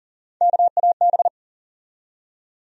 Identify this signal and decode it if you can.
Morse code, heard as a keyed tone, about 40 wpm, 700 Hz. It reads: KAB
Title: USA Call Sign Postfixes - 3 Letters 40wpm